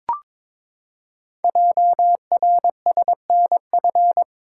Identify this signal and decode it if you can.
Morse code, heard as a keyed tone, about 22 wpm, 700 Hz. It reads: JRSNF
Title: Code Groups - 5 Letters - 22wpm